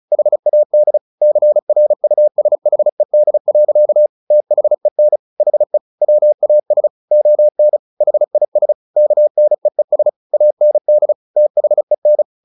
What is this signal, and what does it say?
Morse code, heard as a keyed tone, about 34 wpm, 600 Hz. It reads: HAD CRUSHED. THEN HE WAS ON HIS KNEES AND THEN
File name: the old man and the sea 2 35wpm